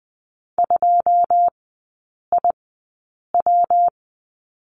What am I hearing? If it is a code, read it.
Morse code, heard as a keyed tone, about 20 wpm, spaced as for 8 wpm, 700 Hz. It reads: 2IW
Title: 178 Lesson 29e - Slash - Three Characters 20wpm